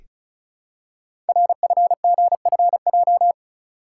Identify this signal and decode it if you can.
Morse code, heard as a keyed tone, about 36 wpm, 700 Hz. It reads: RFGFJ
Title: Code Groups - 5 Letters - 35wpm